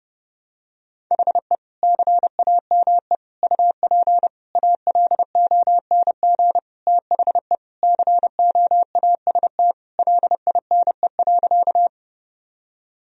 Morse code, heard as a keyed tone, about 30 wpm, 700 Hz. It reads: HE CAME UP ALONG THE COAST LINE.
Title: Sentences from Top 1000 Words - Review Part 4 - 30wpm